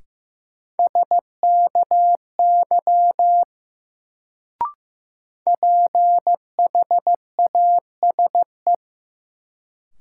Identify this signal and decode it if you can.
Morse code, heard as a keyed tone, about 15 wpm, 700 Hz. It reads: SKY PHASE
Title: Sentences from Top 1200 Words - New Word Drill - 15wpm